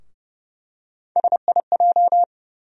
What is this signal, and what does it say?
Morse code, heard as a keyed tone, about 30 wpm, 700 Hz. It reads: SIJ
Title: USA Call Sign Postfixes - 3 Letters 30wpm